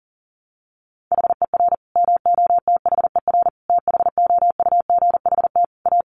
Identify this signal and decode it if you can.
Morse code, heard as a keyed tone, about 40 wpm, 700 Hz. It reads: HER MOTHER THOUGHT A
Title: Sentences from Top 800 Words 40wpm